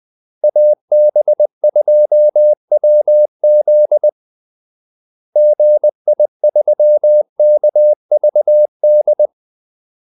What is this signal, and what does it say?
Morse code, heard as a keyed tone, about 20 wpm, 600 Hz. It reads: AB2WZ GI3KVD
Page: Kaldesignaler 20 wpm